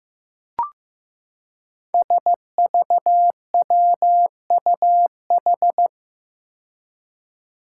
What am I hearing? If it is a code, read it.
Morse code, heard as a keyed tone, about 15 wpm, 700 Hz. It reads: SVWUH
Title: Five Challenging Letter-Number Combinations 15wpm